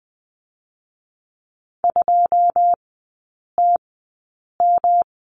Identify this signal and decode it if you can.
Morse code, heard as a keyed tone, about 20 wpm, 700 Hz. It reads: 2 T M
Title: 114 Lesson 19e - Letter w - Three Characters 20wpm